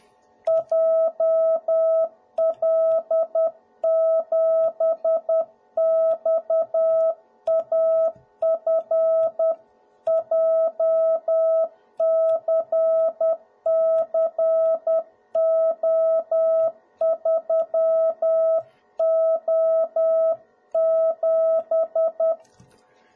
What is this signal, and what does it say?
Morse code, heard as a keyed tone, about 10 wpm, 650 Hz. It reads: JL7XAF JCCO3O7
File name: jh8cbh / 2021cw / cw01b